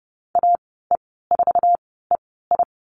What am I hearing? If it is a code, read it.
Morse code, heard as a keyed tone, about 30 wpm, spaced as for 17 wpm, 700 Hz. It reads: AE4EI
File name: Lesson 14g - Number 2 - Call Signs 30wpm